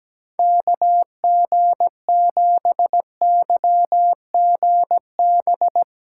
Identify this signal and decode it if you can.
Morse code, heard as a keyed tone, about 17 wpm, 700 Hz. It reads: KG7YGB
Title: USA Callsign DE Callsign 17wpm